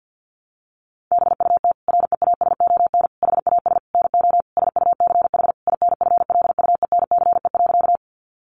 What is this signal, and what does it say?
Morse code, heard as a keyed tone, about 50 wpm, 700 Hz. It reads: BUT REASON HAS NO SUCH INFLUENCE.
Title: Sentences from Top 1100 Words 50wpm